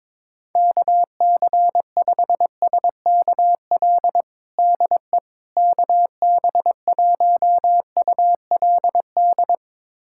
Morse code, heard as a keyed tone, about 22 wpm, 700 Hz. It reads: KC5SKL DE KB1ULD
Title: USA Callsign DE Callsign - 6 Characters - 22wpm